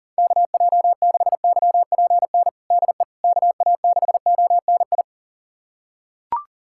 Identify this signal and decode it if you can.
Morse code, heard as a keyed tone, about 40 wpm, 690 Hz. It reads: KJ6YPN DE KA6ONI
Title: USA Callsign DE Callsign - 6 Characters - 40wpm